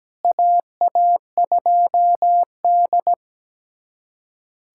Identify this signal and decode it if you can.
Morse code, heard as a keyed tone, about 17 wpm, 700 Hz. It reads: AA2D
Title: USA Callsign DE Callsign - 4 Characters - 17wpm